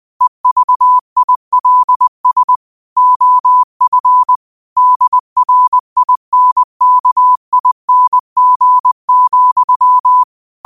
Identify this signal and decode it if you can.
Morse code, heard as a keyed tone, about 20 wpm, 1000 Hz. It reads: EVILS OF DRINKING,
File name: qotdfiles / 20wpm